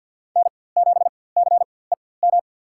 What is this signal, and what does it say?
Morse code, heard as a keyed tone, about 51 wpm, spaced as for 23 wpm, 700 Hz. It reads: N7CEM
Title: USA Call Signs - 5 Characters - 4x Character Spacing - 50wpm